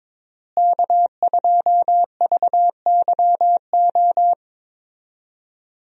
Morse code, heard as a keyed tone, about 22 wpm, 700 Hz. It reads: K2VYO
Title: USA Call Signs - 5 Characters - Repeated 3x - 22wpm